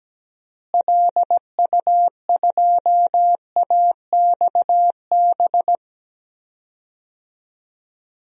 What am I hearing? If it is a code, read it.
Morse code, heard as a keyed tone, about 17 wpm, 700 Hz. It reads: LU2AXB
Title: DXCC-100 Call Signs - 6 Characters - Repeated 3x - 17wpm